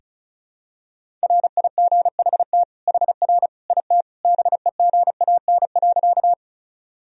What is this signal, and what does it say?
Morse code, heard as a keyed tone, about 36 wpm, 700 Hz. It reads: RIGHT HR IT BEGAN.
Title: Sentences from Top 300 Words and Abbreviations 35wpm